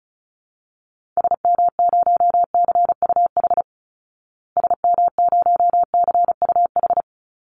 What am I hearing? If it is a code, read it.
Morse code, heard as a keyed tone, about 34 wpm, 700 Hz. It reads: SM0CUH SM0CUH
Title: Call Signs Repeated 3x - 35wpm